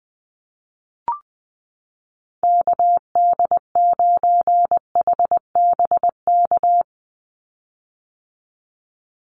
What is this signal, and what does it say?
Morse code, heard as a keyed tone, about 20 wpm, 700 Hz. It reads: KD9HBK
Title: USA Call Signs 20wpm